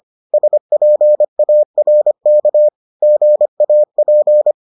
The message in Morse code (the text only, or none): SPARK GAP